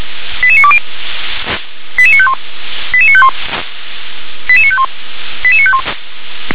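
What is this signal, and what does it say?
Five-tone selective call (ZVEI), heard as quick runs of five tones, tones of about 70 ms. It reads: 88920 88951 88951 88951 88951
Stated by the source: Received by SDR, ZVEI five-tone.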